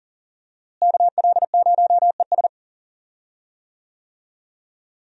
Morse code, heard as a keyed tone, about 40 wpm, 700 Hz. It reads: KL0ES
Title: USA Call Signs - 5 Characters - Repeated 2x - 40wpm